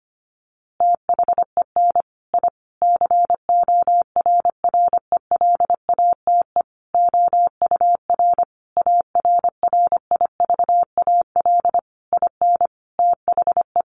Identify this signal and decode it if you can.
Morse code, heard as a keyed tone, about 25 wpm, 700 Hz. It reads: THEN I CORRELATE OUR ARRIVAL IN THE
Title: A Sound Of Thunder 25wpm